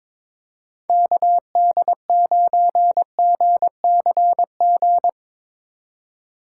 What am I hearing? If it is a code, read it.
Morse code, heard as a keyed tone, about 23 wpm, 700 Hz. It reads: KD9GCG
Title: Speed Racing - USA 2x3 Call Signs 22wpm